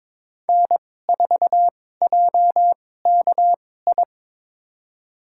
Morse code, heard as a keyed tone, about 22 wpm, spaced as for 16 wpm, 700 Hz. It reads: N4JKI
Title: USA Call Signs - 5 Characters - 2x Character Spacing - 22wpm